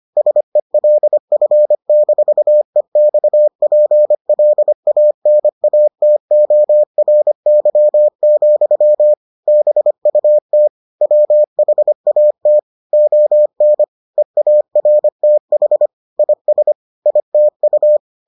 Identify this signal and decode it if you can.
Morse code, heard as a keyed tone, about 25 wpm, 600 Hz. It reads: SELF-EXPLANATORY, BUT WHAT ON EARTH IS ITU